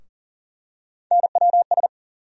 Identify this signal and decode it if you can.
Morse code, heard as a keyed tone, about 38 wpm, 700 Hz. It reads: NWS
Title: USA Call Sign Postfixes - 3 Letters 40wpm